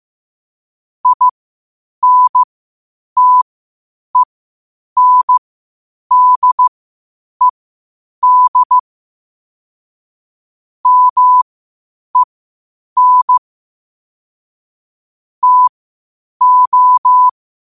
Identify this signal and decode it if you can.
Morse code, heard as a keyed tone, about 15 wpm, spaced as for 8 wpm, 1000 Hz. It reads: INTENDED MEN TO
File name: qotdfiles / 8wpm